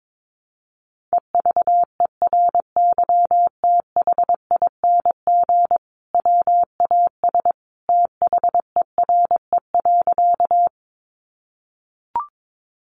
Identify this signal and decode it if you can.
Morse code, heard as a keyed tone, about 22 wpm, 700 Hz. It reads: EVERYTHING WAS THERE.
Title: Sentences from Top 700 Words 22wpm